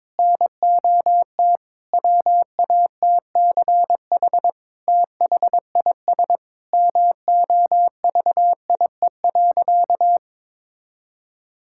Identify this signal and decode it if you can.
Morse code, heard as a keyed tone, about 22 wpm, 700 Hz. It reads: NOT WATCH THIS MOVIE.
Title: Sentences from Top 1000 Words - Review Part 5 - 22wpm